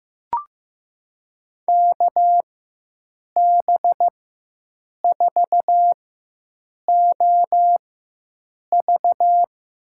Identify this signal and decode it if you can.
Morse code, heard as a keyed tone, about 15 wpm, spaced as for 7 wpm, 700 Hz. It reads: KB4OV
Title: USA Call Signs - 6 Characters - 4x Character Spacing - 15wpm